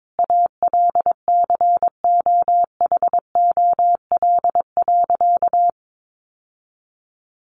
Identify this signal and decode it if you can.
Morse code, heard as a keyed tone, about 22 wpm, 700 Hz. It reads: ALCOHOL.
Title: Sentences from Top 1400 Words 22wpm